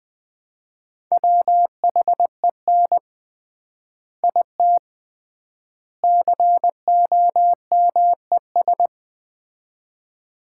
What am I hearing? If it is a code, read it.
Morse code, heard as a keyed tone, about 20 wpm, 700 Hz. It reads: WHEN IT COMES
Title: 3x Word-Spacing - Sets of 3 Words - 20wpm